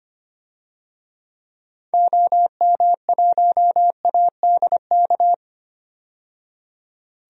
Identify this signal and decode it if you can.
Morse code, heard as a keyed tone, about 25 wpm, 700 Hz. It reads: OM1ADK
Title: DXCC-100 Call Signs - 6 Characters - 25wpm